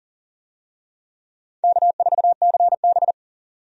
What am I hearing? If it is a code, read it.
Morse code, heard as a keyed tone, about 40 wpm, 700 Hz. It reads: K4CB
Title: USA Call Signs - 4 Characters - 40wpm